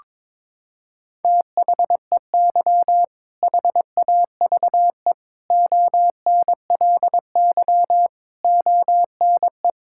Morse code, heard as a keyed tone, about 22 wpm, 700 Hz. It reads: THEY HAVE ONLY ONE
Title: Sentences from Top 1600 Words 22wpm